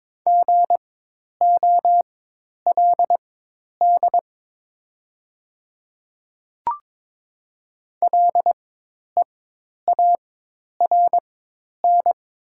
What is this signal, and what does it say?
Morse code, heard as a keyed tone, about 22 wpm, spaced as for 10 wpm, 700 Hz. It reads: GOLD LEARN
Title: 4x Character Spacing - Top 500 Words - 22wpm